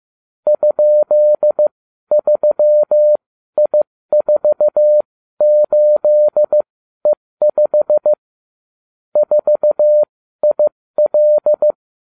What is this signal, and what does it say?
Morse code, heard as a keyed tone, about 15 wpm, spaced as for 12 wpm, 600 Hz. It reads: ?3I48E5 4IL